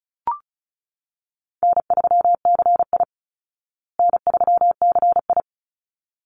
Morse code, heard as a keyed tone, about 34 wpm, 700 Hz. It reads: N3CI N3CI
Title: Easy Call Signs Repeated 2x - 35wpm